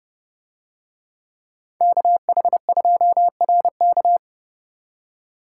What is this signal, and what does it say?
Morse code, heard as a keyed tone, about 29 wpm, 700 Hz. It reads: KH2RK